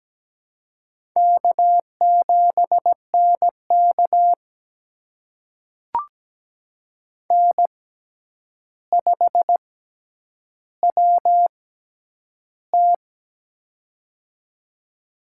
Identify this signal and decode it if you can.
Morse code, heard as a keyed tone, about 17 wpm, spaced as for 9 wpm, 700 Hz. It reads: K7NK N5WT
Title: USA Call Signs - 4 Characters - 6x Character Spacing - 17wpm